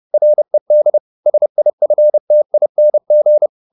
Morse code, heard as a keyed tone, about 30 wpm, 600 Hz. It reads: RED SIFTING
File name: the old man and the sea / the old man and the sea 1 30wpm